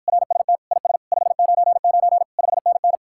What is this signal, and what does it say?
Morse code, heard as a keyed tone, about 52 wpm, 700 Hz. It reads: RST IS 599 5NN